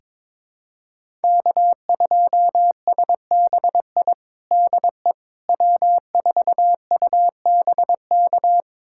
Morse code, heard as a keyed tone, about 22 wpm, 700 Hz. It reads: K2SBI DE W4UBK
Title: USA Callsign DE Callsign - 5 Characters - 22wpm